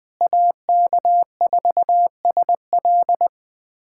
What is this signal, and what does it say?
Morse code, heard as a keyed tone, about 20 wpm, 700 Hz. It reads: AK4SL
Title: USA Callsign DE Callsign - 5 Characters - 20wpm